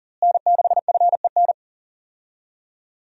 Morse code, heard as a keyed tone, about 39 wpm, 700 Hz. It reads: N6FEN
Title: USA Call Signs - 5 Characters - Repeated 3x - 40wpm